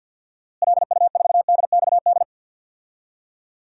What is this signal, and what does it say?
Morse code, heard as a keyed tone, about 52 wpm, 700 Hz. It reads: LU4DXD
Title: DXCC-100 Call Signs - 6 Characters - Repeated 3x - 50wpm